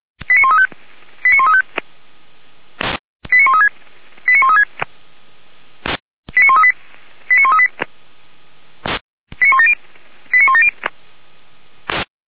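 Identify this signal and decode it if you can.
Five-tone selective call (ZVEI), heard as quick runs of five tones, tones of about 70 ms. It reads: 89135 89135 89136 89136 89138 89138 89170 89170
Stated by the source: ZVEI five-tone selective call received on an SDR, from the Feuerwehr Orlamünde audio server.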